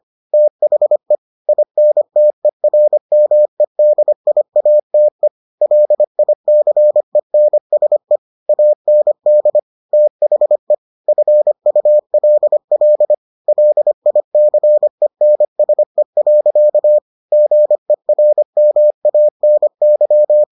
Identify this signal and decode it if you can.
Morse code, heard as a keyed tone, about 25 wpm, 600 Hz. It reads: THE INTERMEDIATE LICENSE AND THE FULL LICENSE. GERMANY